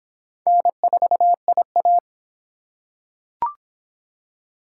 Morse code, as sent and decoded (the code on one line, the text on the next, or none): -. ....- .. .-
N4IA